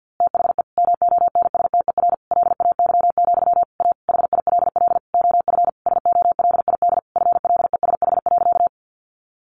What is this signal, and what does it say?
Morse code, heard as a keyed tone, about 48 wpm, 700 Hz. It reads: THE MONSTER LAY, A HILL OF SOLID FLESH.
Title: A Sound Of Thunder 50wpm